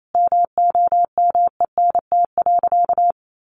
Morse code, heard as a keyed tone, about 28 wpm, 700 Hz. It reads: MOMENT.
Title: Sentences from Top 1400 Words 28wpm